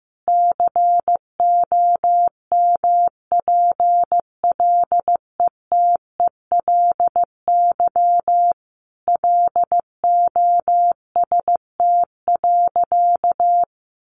Morse code, heard as a keyed tone, about 14 wpm, 700 Hz. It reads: COMPLETELY LOST.
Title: Additional Easy Phrases 15wpm